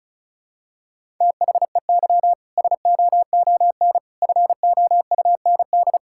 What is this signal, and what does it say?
Morse code, heard as a keyed tone, about 36 wpm, 700 Hz. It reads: THEY SOON FOUND
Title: Sentences from Top 1000 Words - Review Part 5 - 35wpm